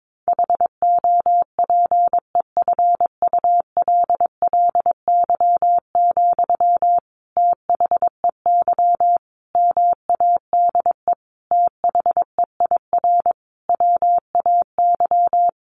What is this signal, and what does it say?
Morse code, heard as a keyed tone, about 22 wpm, 700 Hz. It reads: HOPEFULLY, THEY MADE THEIR WAY
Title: Sentences from Top 1500 Words 22wpm